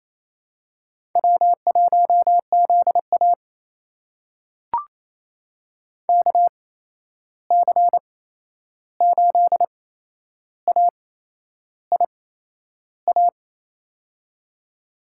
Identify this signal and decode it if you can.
Morse code, heard as a keyed tone, about 28 wpm, spaced as for 11 wpm, 700 Hz. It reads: W1ZA KC8AIA
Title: USA Call Signs - 8x Character Spacing - 28wpm